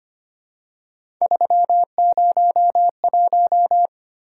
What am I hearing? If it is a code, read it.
Morse code, heard as a keyed tone, about 25 wpm, 700 Hz. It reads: 301
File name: Three Letter-Number Combinations - 25wpm